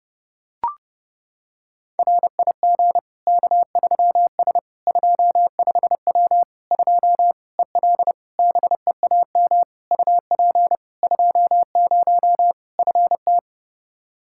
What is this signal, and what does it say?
Morse code, heard as a keyed tone, about 30 wpm, 700 Hz. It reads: RIG K3S 25W 2 EL BEAM UP 20 FT
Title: Practice QSO - Exchange 2 - Info - 30wpm